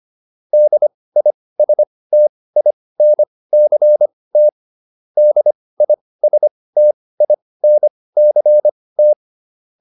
Morse code, heard as a keyed tone, about 25 wpm, spaced as for 18 wpm, 600 Hz. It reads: DISTINCT DISTINCT
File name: DIS8 18